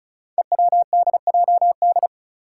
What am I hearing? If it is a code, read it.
Morse code, heard as a keyed tone, about 36 wpm, 700 Hz. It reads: EWDJD
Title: Five Challenging Letter-Number Combinations 35wpm